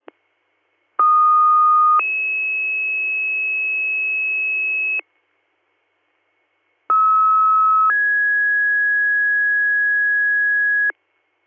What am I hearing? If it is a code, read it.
Two-tone page, heard as a one-second tone, then a three-second tone, about 1230 then 2600 Hz, about 1290 then 1650 Hz. It is heard from 1.0 to 10.9 s.